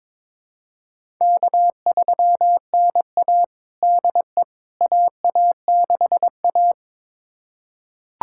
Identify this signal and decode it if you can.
Morse code, heard as a keyed tone, about 21 wpm, 700 Hz. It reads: K3NA DE AA6A
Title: USA Callsign DE Callsign - 4 Characters - 22wpm